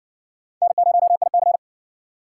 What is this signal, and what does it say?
Morse code, heard as a keyed tone, about 53 wpm, 700 Hz. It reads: N0IC